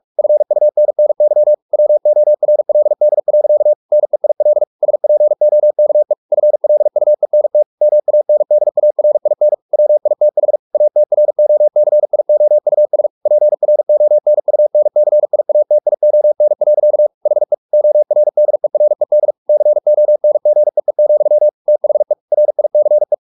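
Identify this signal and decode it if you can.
Morse code, heard as a keyed tone, about 44 wpm, 600 Hz. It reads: FUNNY WORLD. NEIL SPOKE FLUENT MANDARIN WITH ATROCIOUS PRONUNCIATION. HE ORDERED CONGEE, THE RICE
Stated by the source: Morse practice at 45 wpm, the novel Never by Ken Follett in English, from CW med Gnister.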